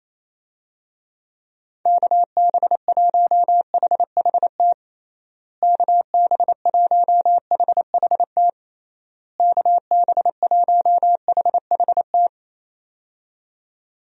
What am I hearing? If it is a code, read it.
Morse code, heard as a keyed tone, about 28 wpm, 700 Hz. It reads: KB1HHT KB1HHT KB1HHT
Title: USA Call Signs - 6 Characters - Repeated 3x - 28wpm